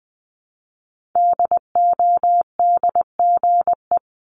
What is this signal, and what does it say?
Morse code, heard as a keyed tone, about 20 wpm, 700 Hz. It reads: DODGE
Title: Multi-letter Phonemes Repeated 3x 20wpm